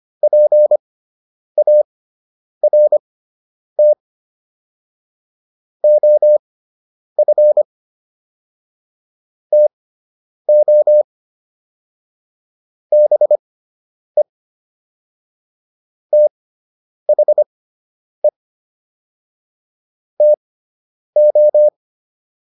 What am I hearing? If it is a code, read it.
Morse code, heard as a keyed tone, about 25 wpm, spaced as for 9 wpm, 600 Hz. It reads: PART OF TO BE THE TO